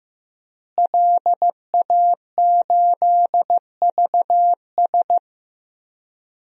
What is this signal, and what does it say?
Morse code, heard as a keyed tone, about 15 wpm, 700 Hz. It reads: LA8VS
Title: DXCC-100 Call Signs - 5 Characters - Repeated 3x - 15wpm